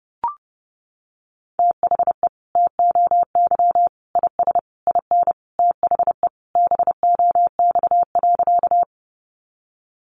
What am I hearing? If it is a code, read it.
Morse code, heard as a keyed tone, about 30 wpm, 700 Hz. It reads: THE TOY IS IN THE BOX.
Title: Easy Phrases 30wpm